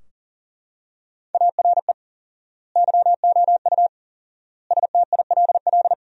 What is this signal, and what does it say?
Morse code, heard as a keyed tone, about 40 wpm, 700 Hz. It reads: ARE YOU STILL
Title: 4x Word-Spacing - Sentences from Top 400 Words - 40wpm